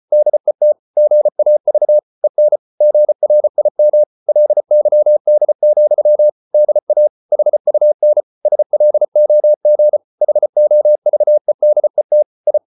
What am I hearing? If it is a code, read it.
Morse code, heard as a keyed tone, about 34 wpm, 600 Hz. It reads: DET GAV EN GRIM LYD, DA HUN SLOG HOVEDET I
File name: Ferie QRQ 34wpm